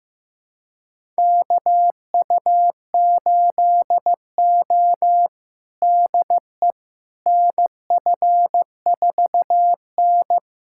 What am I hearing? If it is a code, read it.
Morse code, heard as a keyed tone, about 15 wpm, 700 Hz. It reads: KU8O DE NF4N
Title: USA Callsign DE Callsign - 4 Characters - 15wpm